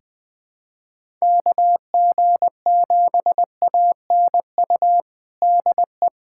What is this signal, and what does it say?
Morse code, heard as a keyed tone, about 20 wpm, 700 Hz. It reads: KG7ANU DE
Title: USA Callsign DE Callsign - 6 Characters - 20wpm